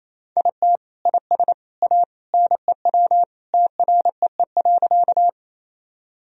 Morse code, heard as a keyed tone, about 27 wpm, 700 Hz. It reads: IT IS A NEW TREE.
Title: Sentences from Top 1000 Words - Review Part 1 - 28wpm